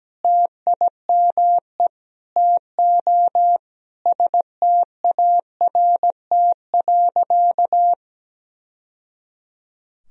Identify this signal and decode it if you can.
Morse code, heard as a keyed tone, about 17 wpm, 700 Hz. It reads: TIME TO START.
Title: Sentences from Top 1000 Words - Review Part 1 - 17wpm